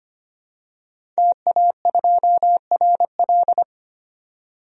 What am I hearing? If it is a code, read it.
Morse code, heard as a keyed tone, about 25 wpm, 700 Hz. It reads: TA2RL
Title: DXCC-100 Call Signs - 5 Characters - 25wpm